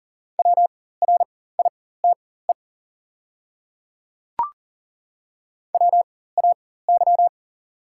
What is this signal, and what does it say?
Morse code, heard as a keyed tone, about 40 wpm, spaced as for 19 wpm, 690 Hz. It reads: WRITE WAY
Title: 4x Character Spacing - Top 100 Words - 40wpm